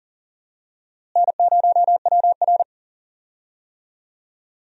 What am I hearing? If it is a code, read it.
Morse code, heard as a keyed tone, about 40 wpm, 700 Hz. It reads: N0WR
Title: USA Call Signs - 4 Characters - Repeated 2x - 40wpm